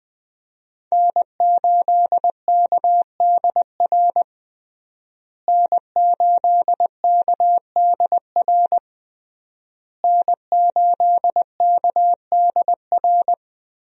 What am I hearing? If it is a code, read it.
Morse code, heard as a keyed tone, about 21 wpm, 700 Hz. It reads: N8KDR N8KDR N8KDR
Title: USA Call Signs - 5 Characters - Repeated 3x - 20wpm